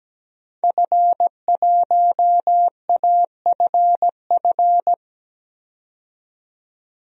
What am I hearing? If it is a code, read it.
Morse code, heard as a keyed tone, about 17 wpm, 700 Hz. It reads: F1AFF
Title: DXCC-100 Call Signs - 5 Characters - 17wpm